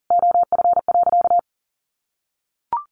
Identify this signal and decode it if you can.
Morse code, heard as a keyed tone, about 40 wpm, 700 Hz. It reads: OF.
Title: Sentences from Top 1300 Words 40wpm